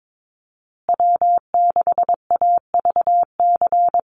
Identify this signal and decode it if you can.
Morse code, heard as a keyed tone, about 22 wpm, 700 Hz. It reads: W6AVC
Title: USA Call Signs Repeated 3x 22wpm